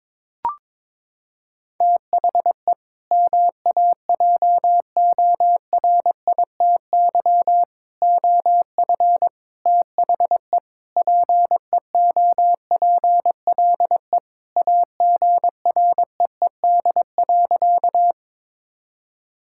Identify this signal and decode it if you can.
Morse code, heard as a keyed tone, about 21 wpm, 700 Hz. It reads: THE MAJORITY OF THE PEOPLE AGREED.
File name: practice / 22wpm / Sentences with 4-Word Phrases - 22wpm